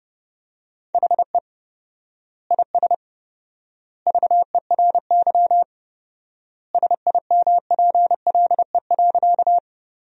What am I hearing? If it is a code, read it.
Morse code, heard as a keyed tone, about 30 wpm, 700 Hz. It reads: HE IS VERY SIMPLE.
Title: 4x Word-Spacing - Sentences from Top 400 Words - 30wpm